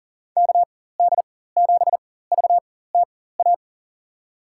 Morse code, heard as a keyed tone, about 40 wpm, spaced as for 19 wpm, 700 Hz. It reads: KD7VTA
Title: USA Call Signs - 4x Character Spacing - 40wpm